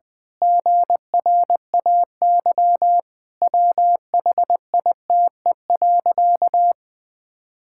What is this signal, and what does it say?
Morse code, heard as a keyed tone, about 20 wpm, 700 Hz. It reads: GRAY WHITE.
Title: Pangram Challenge v2 - 20wpm